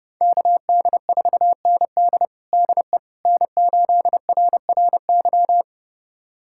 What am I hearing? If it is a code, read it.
Morse code, heard as a keyed tone, about 30 wpm, 700 Hz. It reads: KD4ND DE N8RRY
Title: USA Callsign DE Callsign - 5 Characters - 30wpm